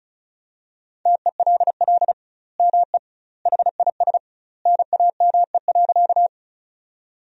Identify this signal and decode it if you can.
Morse code, heard as a keyed tone, about 36 wpm, 700 Hz. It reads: TELL ME HIS NAME.
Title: 2x Word-Spacing - Sentences from Top 200 Words - 35wpm